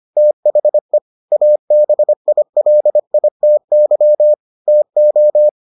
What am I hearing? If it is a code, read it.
Morse code, heard as a keyed tone, about 25 wpm, 600 Hz. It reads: THE ABILITY TO